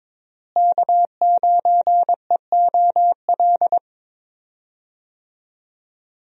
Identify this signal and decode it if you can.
Morse code, heard as a keyed tone, about 22 wpm, 700 Hz. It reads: K9EOL
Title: USA Call Signs - 5 Characters - Repeated 2x - 22wpm